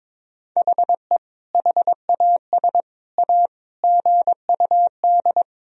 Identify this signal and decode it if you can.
Morse code, heard as a keyed tone, about 22 wpm, 700 Hz. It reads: HE HAS A GUD
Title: Sentences from Top 500 Words and Abbreviations 22wpm